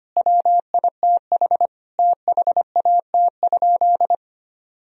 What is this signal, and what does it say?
Morse code, heard as a keyed tone, about 25 wpm, 700 Hz. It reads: WITH THAT?
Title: Sentences from Top 1000 Words - Review Part 4 - 25wpm